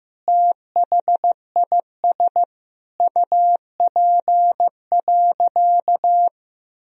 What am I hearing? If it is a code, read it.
Morse code, heard as a keyed tone, about 15 wpm, 700 Hz. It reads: THIS UP.
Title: Sentences from Top 1000 Words - Review Part 1 - 15wpm